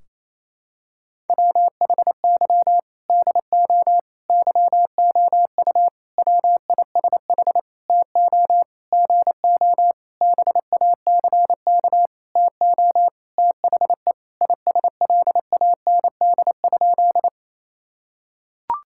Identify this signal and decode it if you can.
Morse code, heard as a keyed tone, about 27 wpm, 700 Hz. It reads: WHY DO YOU WISH TO GO BACK TO THE ISLAND?
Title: Sentences from Top 1600 Words 28wpm